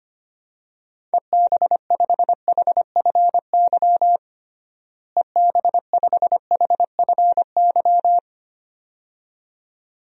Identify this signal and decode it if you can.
Morse code, heard as a keyed tone, about 25 wpm, 700 Hz. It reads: EB5HFY EB5HFY
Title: DXCC-100 Call Signs - 6 Characters - Repeated 2x - 25wpm